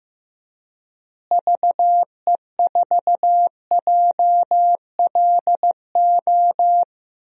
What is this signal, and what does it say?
Morse code, heard as a keyed tone, about 15 wpm, 700 Hz. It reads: VE4JLO
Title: DXCC-100 Call Signs - 6 Characters - Repeated 2x - 15wpm